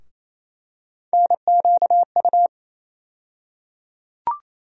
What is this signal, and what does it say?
Morse code, heard as a keyed tone, about 28 wpm, 700 Hz. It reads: NQU
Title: USA Call Sign Postfixes - 3 Letters 28wpm